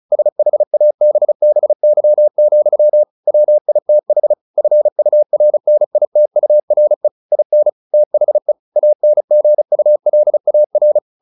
Morse code, heard as a keyed tone, about 34 wpm, 600 Hz. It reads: SHABBY, WITH FURNITURE IN THE ANGULAR